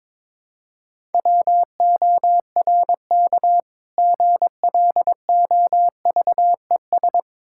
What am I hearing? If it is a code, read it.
Morse code, heard as a keyed tone, about 22 wpm, 690 Hz. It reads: WORK GLOVES
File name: Difficult Words with Hints - 22wpm